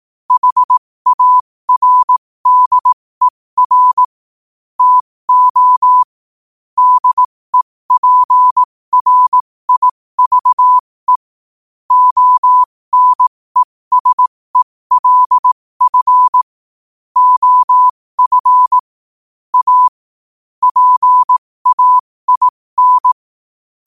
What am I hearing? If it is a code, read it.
Morse code, heard as a keyed tone, about 18 wpm, spaced as for 15 wpm, 1000 Hz. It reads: HARDER TO DEPRIVE ONESELF OF A PAIN
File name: qotdfiles / 15wpm